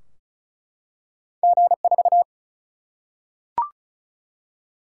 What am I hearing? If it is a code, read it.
Morse code, heard as a keyed tone, about 36 wpm, 700 Hz. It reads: G 4
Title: Single-Letter Single-Number 35wpm